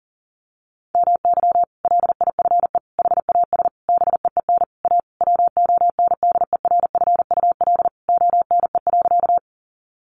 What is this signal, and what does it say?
Morse code, heard as a keyed tone, about 40 wpm, 700 Hz. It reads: MY LIFE HAS BEEN A WONDERFUL ONE.
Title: Sentences from Top 1100 Words 40wpm